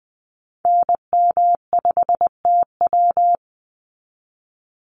Morse code, heard as a keyed tone, about 20 wpm, 700 Hz. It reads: NM5TW
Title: NA SOTA Activator Callsigns 20wpm